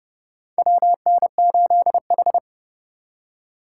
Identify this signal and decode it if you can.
Morse code, heard as a keyed tone, about 30 wpm, 700 Hz. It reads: WN8H
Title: USA Call Signs - 4 Characters - 4x Character Spacing - 30wpm